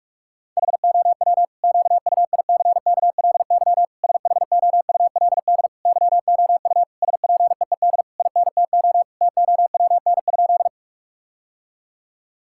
Morse code, heard as a keyed tone, about 42 wpm, 700 Hz. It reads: HOW QUICKLY SHOULD YOU SPEED INTO TOWN?
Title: Sentences from Top 1000 Words - Review Part 5 - 45wpm